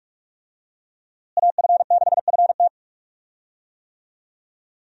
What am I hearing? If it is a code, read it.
Morse code, heard as a keyed tone, about 43 wpm, 700 Hz. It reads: AF6FT
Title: USA Call Signs - 5 Characters - Repeated 3x - 45wpm